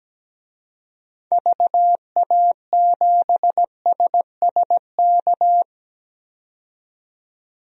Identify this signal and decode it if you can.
Morse code, heard as a keyed tone, about 17 wpm, 700 Hz. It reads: VA7SSK
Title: DXCC-100 Call Signs - 6 Characters - 17wpm